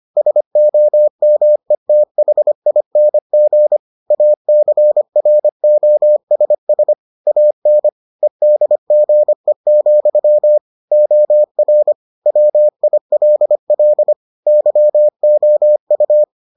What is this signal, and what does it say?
Morse code, heard as a keyed tone, about 25 wpm, 600 Hz. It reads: SOMETHING ACROSS AN EDGE, OR WILL YOU